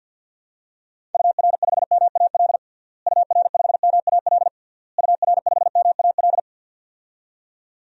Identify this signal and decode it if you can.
Morse code, heard as a keyed tone, about 52 wpm, 700 Hz. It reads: UR5MAL UR5MAL UR5MAL
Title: DXCC-100 Call Signs - 6 Characters - Repeated 3x - 50wpm